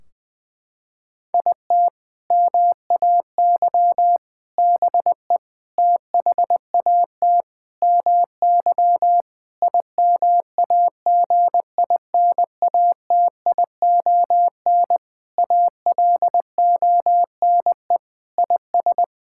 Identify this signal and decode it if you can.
Morse code, heard as a keyed tone, about 20 wpm, 700 Hz. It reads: IT MAY BE THAT MY IMAGINATION ALONE IS
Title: Sentences from Top 1600 Words 20wpm